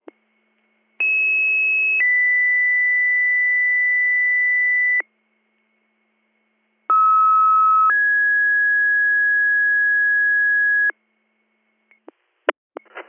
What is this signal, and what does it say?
Two-tone page, heard as a one-second tone, then a three-second tone, about 2600 then 2000 Hz, about 1290 then 1650 Hz. It is heard from 1.0 to 10.9 s.